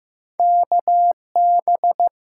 Morse code, heard as a keyed tone, about 15 wpm, 700 Hz. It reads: KB
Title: USA Call Signs - 8x Character Spacing - 15wpm